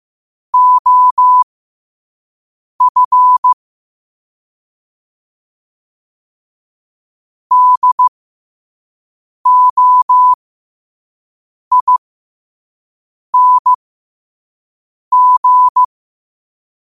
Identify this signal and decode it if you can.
Morse code, heard as a keyed tone, about 15 wpm, spaced as for 5 wpm, 1000 Hz. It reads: OF DOING
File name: qotdfiles / 5wpm